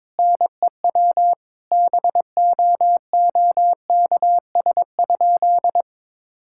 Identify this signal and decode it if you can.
Morse code, heard as a keyed tone, about 22 wpm, 700 Hz. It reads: NEW BOOKS?
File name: Sentences with 9-Word Phrases - 22wpm